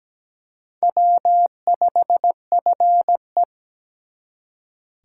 Morse code, heard as a keyed tone, about 20 wpm, 700 Hz. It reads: W5FE
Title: Speed Racing - USA Call Signs 17wpm